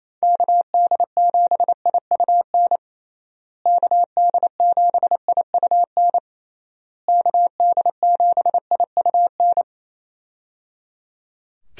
Morse code, heard as a keyed tone, about 27 wpm, 700 Hz. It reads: KD7IUN KD7IUN KD7IUN
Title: USA Call Signs - 6 Characters - Repeated 3x - 28wpm